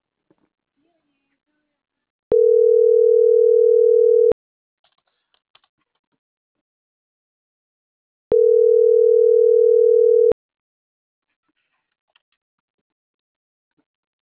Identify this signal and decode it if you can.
Ringback tone (North America), 440 + 480 Hz.